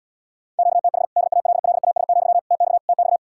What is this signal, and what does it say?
Morse code, heard as a keyed tone, about 80 wpm, 700 Hz. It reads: QTH DALLAS, TX TX